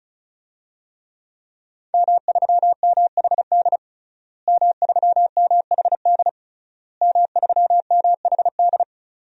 Morse code, heard as a keyed tone, about 36 wpm, 700 Hz. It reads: M3MHD M3MHD M3MHD
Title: DXCC-100 Call Signs - 5 Characters - Repeated 3x - 35wpm